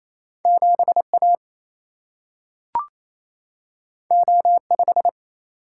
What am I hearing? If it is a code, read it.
Morse code, heard as a keyed tone, about 28 wpm, 700 Hz. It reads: 7A O5
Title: Single-Letter Single-Number 28wpm